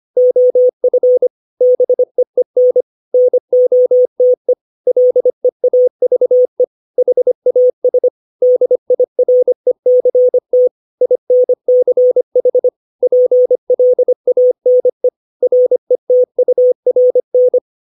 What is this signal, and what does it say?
Morse code, heard as a keyed tone, about 25 wpm, 500 Hz. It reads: OF BEEN NOTE LEAVE HAS DIRECT INCH PLANE RETURN
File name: 632 words / 632 Words 25WPM